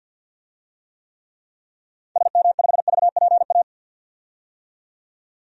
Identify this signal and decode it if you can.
Morse code, heard as a keyed tone, about 50 wpm, 700 Hz. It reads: SM5VPA
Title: DXCC-100 Call Signs - 6 Characters - 50wpm